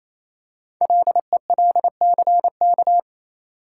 Morse code, heard as a keyed tone, about 27 wpm, 700 Hz. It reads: LELCK
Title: Code Groups - 5 Letters - 28wpm